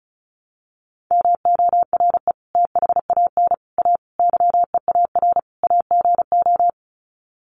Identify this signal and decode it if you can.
Morse code, heard as a keyed tone, about 34 wpm, 700 Hz. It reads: MORE THAN A YEAR AGO
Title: Sets of 5 Words Encore2 35wpm